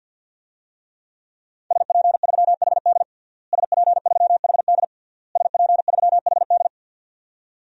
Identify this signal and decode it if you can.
Morse code, heard as a keyed tone, about 49 wpm, 700 Hz. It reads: SP3HD SP3HD SP3HD
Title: DXCC-100 Call Signs - 5 Characters - Repeated 3x - 50wpm